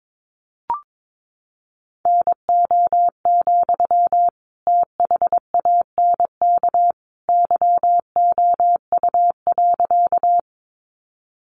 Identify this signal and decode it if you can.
Morse code, heard as a keyed tone, about 22 wpm, 700 Hz. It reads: NO, THANK YOU.
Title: Sentences from Top 600 Words 22wpm